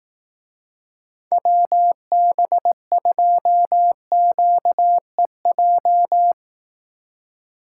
Morse code, heard as a keyed tone, about 18 wpm, 700 Hz. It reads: WB2QEJ
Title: Speed Racing - USA 2x3 Call Signs 15wpm